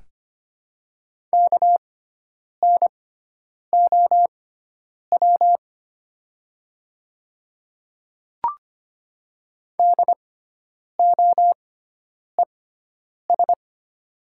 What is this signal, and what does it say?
Morse code, heard as a keyed tone, about 25 wpm, spaced as for 9 wpm, 700 Hz. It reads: KNOW DOES
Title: 6x Character Spacing - Top 500 Words - 25wpm